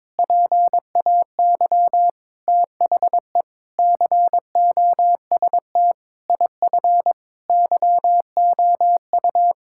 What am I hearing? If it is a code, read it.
Morse code, heard as a keyed tone, about 22 wpm, 700 Hz. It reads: PAY THE COST IF YOU
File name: practice / 22wpm / Sets of 8 Words - 22wpm